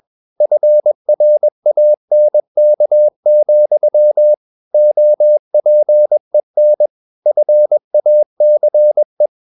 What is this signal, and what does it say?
Morse code, heard as a keyed tone, about 21 wpm, 600 Hz. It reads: FRANK, OPEN FACE
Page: CW med Gnister